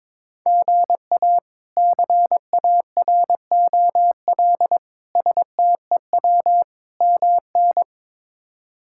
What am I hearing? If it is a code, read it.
Morse code, heard as a keyed tone, about 22 wpm, 700 Hz. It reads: GA CAROL STEW MN
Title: K1USN SST QSO Practice - 22wpm